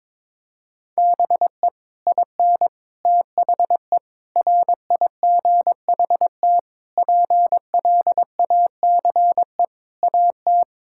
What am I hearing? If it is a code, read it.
Morse code, heard as a keyed tone, about 22 wpm, 700 Hz. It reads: BE IN THE RIGHT PLACE AT
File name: Sets of 9 Words - 22wpm